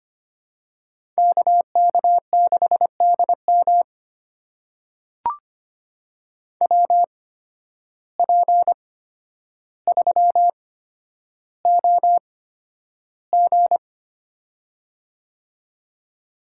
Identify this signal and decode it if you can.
Morse code, heard as a keyed tone, about 25 wpm, spaced as for 11 wpm, 700 Hz. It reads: KK6DM WP3OG
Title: USA Call Signs - 5 Characters - 8x Character Spacing - 25wpm